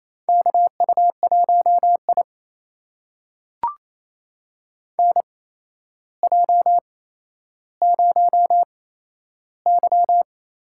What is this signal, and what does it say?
Morse code, heard as a keyed tone, about 28 wpm, spaced as for 12 wpm, 700 Hz. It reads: KU1I NJ0Y
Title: USA Call Signs - 4 Characters - 8x Character Spacing - 28wpm